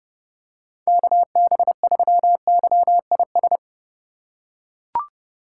Morse code, heard as a keyed tone, about 32 wpm, 700 Hz. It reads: KB3YIS
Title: USA Call Signs - 6 Characters - 3x Character Spacing - 30wpm